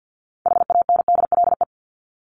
Morse code, heard as a keyed tone, about 50 wpm, 700 Hz. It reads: HANDLE
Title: Top 1001-1250 Words 50wpm